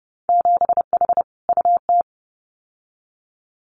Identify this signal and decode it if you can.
Morse code, heard as a keyed tone, about 30 wpm, 700 Hz. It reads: 7H UT
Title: Winter Field Day Exchanges Encore 30wpm